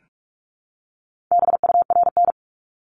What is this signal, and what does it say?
Morse code, heard as a keyed tone, about 44 wpm, 700 Hz. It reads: BURN
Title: Top 1251-1500 Words 45wpm